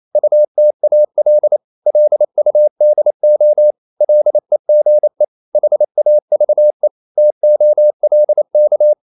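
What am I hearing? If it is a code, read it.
Morse code, heard as a keyed tone, about 28 wpm, 600 Hz. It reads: UTAL LUDO LEGE HAVE TOLK
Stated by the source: Danish short words sent in Morse at 28 wpm, from CW med Gnister.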